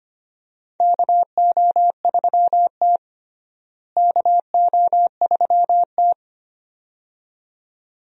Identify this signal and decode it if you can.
Morse code, heard as a keyed tone, about 25 wpm, 700 Hz. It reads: KO3T KO3T
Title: USA Call Signs - 4 Characters - Repeated 2x - 25wpm